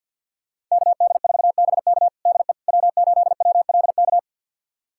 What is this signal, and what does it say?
Morse code, heard as a keyed tone, about 46 wpm, 700 Hz. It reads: KD4BK DE W8WLK
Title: USA Callsign DE Callsign 50wpm